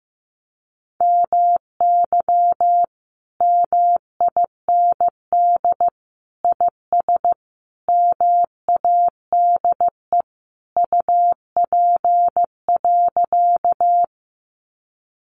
Morse code, heard as a keyed tone, about 14 wpm, 700 Hz. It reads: MY MIND IS MADE UP.
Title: Sentences from Top 500 Words 15wpm